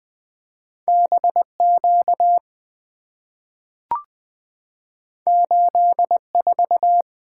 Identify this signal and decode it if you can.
Morse code, heard as a keyed tone, about 20 wpm, 700 Hz. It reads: B Q 8 4
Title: Two Letter-Number Combinations - 20wpm